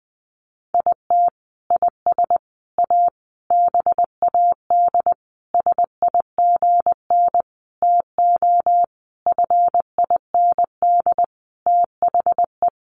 Morse code, heard as a keyed tone, about 20 wpm, 700 Hz. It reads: IT IS A BAD SIGN TO FIND THE
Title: Sentences from Top 1000 Words 20wpm